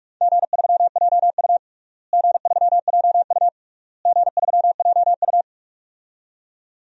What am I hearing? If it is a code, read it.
Morse code, heard as a keyed tone, about 43 wpm, 700 Hz. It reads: G3JU G3JU G3JU
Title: DXCC-100 Call Signs - 4 Characters - Repeated 3x - 45wpm